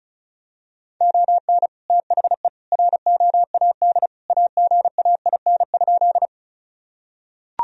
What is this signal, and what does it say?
Morse code, heard as a keyed tone, about 36 wpm, 700 Hz. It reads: ON THE ROAD AGAIN?
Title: Sentences from Top 1000 Words - Review Part 2 - 35wpm